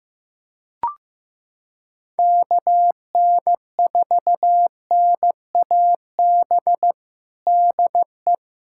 Morse code, heard as a keyed tone, about 15 wpm, 700 Hz. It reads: KN4NAB DE
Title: USA Callsign DE Callsign - 6 Characters - 15wpm